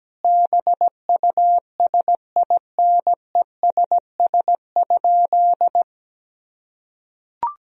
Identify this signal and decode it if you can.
Morse code, heard as a keyed tone, about 17 wpm, 700 Hz. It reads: BUSINESS?
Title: Sentences from Top 1000 Words - Review Part 3 - 17wpm